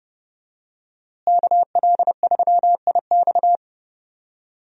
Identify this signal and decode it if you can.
Morse code, heard as a keyed tone, about 29 wpm, 700 Hz. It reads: KL3IX
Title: USA Call Signs - 5 Characters - Repeated 3x - 30wpm